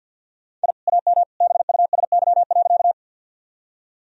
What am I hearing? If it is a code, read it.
Morse code, heard as a keyed tone, about 50 wpm, 700 Hz. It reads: I AM BUSY.
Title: Sentences from Top 1000 Words - Review Part 3 - 50wpm